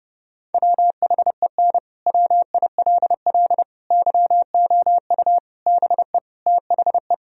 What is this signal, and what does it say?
Morse code, heard as a keyed tone, about 32 wpm, 700 Hz. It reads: WHEN WILL YOU BE THE
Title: Sentences from Top 1000 Words - Review Part 4 - 30wpm